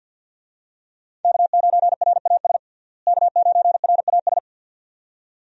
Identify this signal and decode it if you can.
Morse code, heard as a keyed tone, about 52 wpm, 700 Hz. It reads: K9RAS K9RAS
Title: USA Call Signs - 5 Characters - Repeated 2x - 50wpm